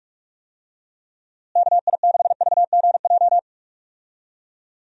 Morse code, heard as a keyed tone, about 45 wpm, 700 Hz. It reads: KI6VGJ
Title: USA Call Signs - 6 Characters - Repeated 2x - 45wpm